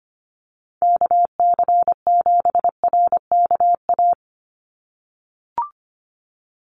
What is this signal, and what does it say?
Morse code, heard as a keyed tone, about 25 wpm, 700 Hz. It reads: KC7RKA
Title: USA Call Signs 25wpm